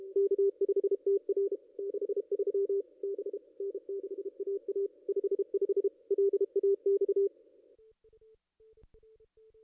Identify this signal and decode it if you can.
Morse code, heard as a keyed tone, about 33 wpm, 440 Hz. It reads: K5TR 63 B N6AA 55 LAX
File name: contests / ss-cw-k5tr-op-n5rz -0500